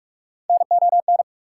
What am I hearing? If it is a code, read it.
Morse code, heard as a keyed tone, about 62 wpm, spaced as for 54 wpm, 700 Hz. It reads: NON
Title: Speed-Racing Prefixes and Suffixes 45wpm